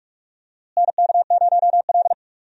Morse code, heard as a keyed tone, about 45 wpm, 700 Hz. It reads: NK0L
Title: USA Call Signs - 4 Characters - Repeated 3x - 45wpm